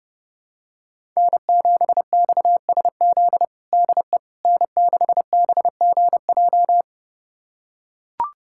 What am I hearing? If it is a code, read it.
Morse code, heard as a keyed tone, about 30 wpm, 700 Hz. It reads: N7XSZ DE N6BGJ
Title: USA Callsign DE Callsign - 5 Characters - 30wpm